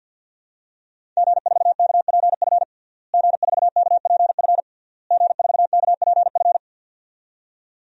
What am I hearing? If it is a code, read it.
Morse code, heard as a keyed tone, about 51 wpm, 700 Hz. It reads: G4KPF G4KPF G4KPF
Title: DXCC-100 Call Signs - 5 Characters - Repeated 3x - 50wpm